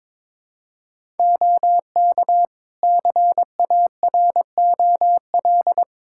Morse code, heard as a keyed tone, about 22 wpm, 700 Hz. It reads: OK CAROL
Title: K1USN SST QSO Practice 22wpm